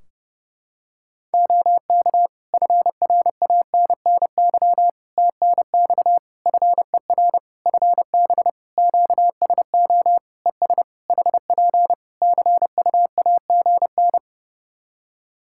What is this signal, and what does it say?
Morse code, heard as a keyed tone, about 32 wpm, 700 Hz. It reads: OK FRANNY TNX FER FB QSO ES HP CUAGN
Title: Practice QSO - Exchange 4 - Ending - 30wpm